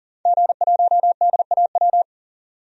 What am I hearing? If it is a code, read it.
Morse code, heard as a keyed tone, about 40 wpm, 700 Hz. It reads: G1DAW
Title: DXCC-100 Call Signs - 5 Characters - 40wpm